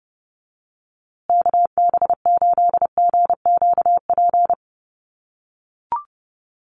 Morse code, heard as a keyed tone, about 30 wpm, 700 Hz. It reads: KB8GQP
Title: USA Call Signs Repeated 3x 30wpm